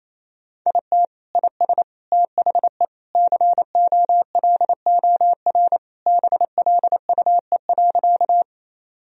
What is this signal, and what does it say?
Morse code, heard as a keyed tone, about 28 wpm, 700 Hz. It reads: IT IS THE COLOR BLUE.
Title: Sentences from Top 1000 Words - Review Part 3 - 28wpm